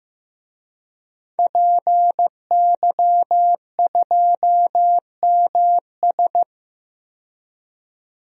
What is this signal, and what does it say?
Morse code, heard as a keyed tone, about 15 wpm, 700 Hz. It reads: PY2MS
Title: DXCC-100 Call Signs - 5 Characters - 15wpm